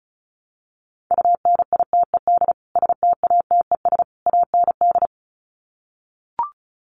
Morse code, heard as a keyed tone, about 34 wpm, 700 Hz. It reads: UNITED STATES AND
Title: Sets of 3 Words 35wpm